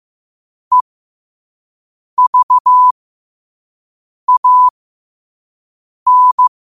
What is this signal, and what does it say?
Morse code, heard as a keyed tone, about 15 wpm, spaced as for 5 wpm, 1000 Hz. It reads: EVAN
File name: 5wpm